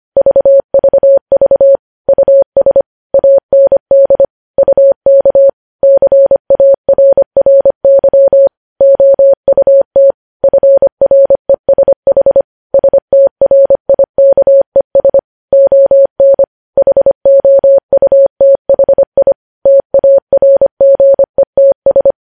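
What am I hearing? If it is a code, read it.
Morse code, heard as a keyed tone, about 25 wpm, 550 Hz. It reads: VVV US AND UK CARRY OUT FRESH STRIKES ON HOUTHI TARGETS
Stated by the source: Morse code practice of news headlines at 25 wpm, opening with vvv.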